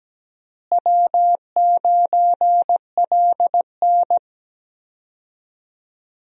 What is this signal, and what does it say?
Morse code, heard as a keyed tone, about 17 wpm, 700 Hz. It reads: W9LN
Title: USA Call Signs - 4 Characters - Repeated 3x - 17wpm